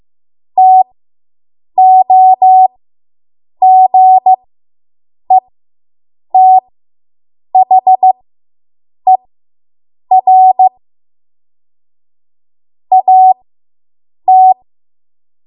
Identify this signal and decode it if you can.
Morse code, heard as a keyed tone, about 15 wpm, spaced as for 7 wpm, 750 Hz. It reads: TOGETHER AT